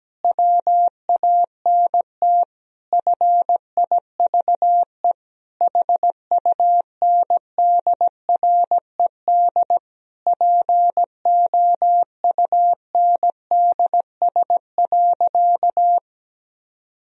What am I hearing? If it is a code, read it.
Morse code, heard as a keyed tone, about 17 wpm, 700 Hz. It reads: WANT FIVE HUNDRED POUNDS.
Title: Sentences from Top 1000 Words - Review Part 2 - 17wpm